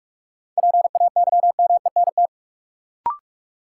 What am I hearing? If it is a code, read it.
Morse code, heard as a keyed tone, about 46 wpm, 700 Hz. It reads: PAYMENT
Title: Sentences from Top 1200 Words - New Word Drill - 45wpm